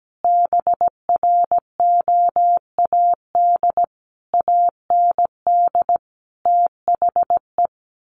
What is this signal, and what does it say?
Morse code, heard as a keyed tone, about 17 wpm, 700 Hz. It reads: BROAD AND THE
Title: A Sound Of Thunder 17wpm